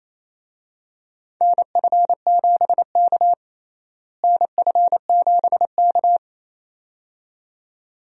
Morse code, heard as a keyed tone, about 28 wpm, 700 Hz. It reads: NF7K NF7K
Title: USA Call Signs - 4 Characters - Repeated 2x - 28wpm